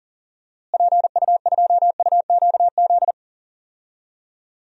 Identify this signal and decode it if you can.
Morse code, heard as a keyed tone, about 40 wpm, 700 Hz. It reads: PU2UQZ
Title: DXCC-100 Call Signs - 6 Characters - Repeated 3x - 40wpm